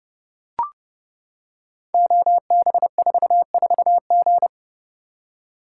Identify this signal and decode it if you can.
Morse code, heard as a keyed tone, about 30 wpm, 700 Hz. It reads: OB44G
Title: Code Groups - 5 Letter:Numbers - 30wpm